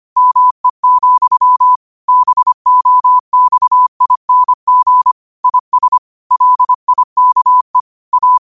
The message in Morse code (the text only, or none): ME, BOXING IS LIKE A